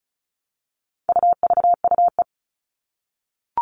Morse code, heard as a keyed tone, about 34 wpm, 700 Hz. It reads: UVUE
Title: All Cut Numbers - 4-Digit 35wpm